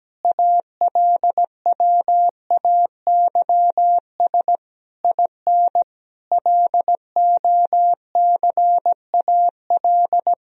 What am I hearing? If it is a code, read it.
Morse code, heard as a keyed tone, about 17 wpm, 700 Hz. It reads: ALWAYS IN LOCAL
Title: Sentences from Top 1000 Words - Review Part 3 - 17wpm